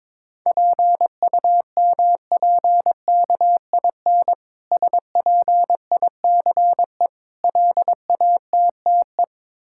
Morse code, heard as a keyed tone, about 22 wpm, 690 Hz. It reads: PUMPKIN SPICE LATTE
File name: Difficult Words with Hints - 22wpm